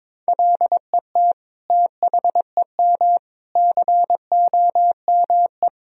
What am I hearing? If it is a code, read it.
Morse code, heard as a keyed tone, about 22 wpm, 700 Hz. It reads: LET THEM COME
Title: Sentences from Top 1000 Words - Review Part 3 - 22wpm